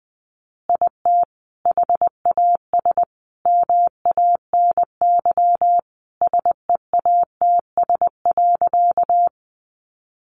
Morse code, heard as a keyed tone, about 20 wpm, 700 Hz. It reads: IT HAS MANY SEATS.
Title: Additional Easy Phrases 20wpm